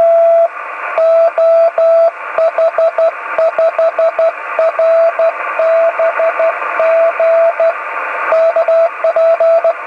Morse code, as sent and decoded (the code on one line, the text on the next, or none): none